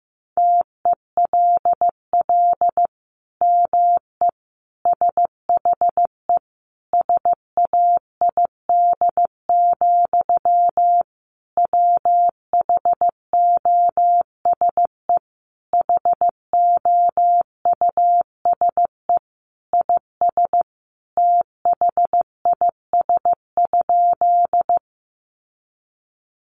Morse code, heard as a keyed tone, about 15 wpm, 700 Hz. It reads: TELL ME SHE SAID, WHOSE HOUSE IS THIS?
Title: Sentences from Top 1200 Words 15wpm